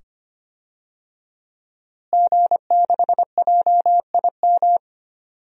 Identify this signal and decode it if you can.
Morse code, heard as a keyed tone, about 25 wpm, 700 Hz. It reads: G6JIM
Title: DXCC-100 Call Signs - 5 Characters - Repeated 2x - 25wpm